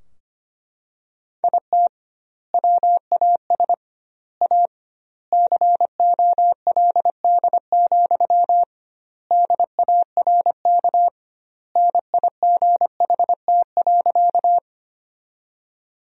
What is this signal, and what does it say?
Morse code, heard as a keyed tone, about 25 wpm, 700 Hz. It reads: IT WAS A COLD, DARK NIGHT.
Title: 2x Word-Spacing - Sentences from Top 500 Words - 25wpm